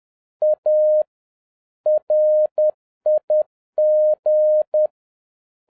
Morse code, heard as a keyed tone, about 10 wpm, 610 Hz. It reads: A RIG
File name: 2011-07-mp3 / n41 EXCITING QRP CW OPERATION 10wpm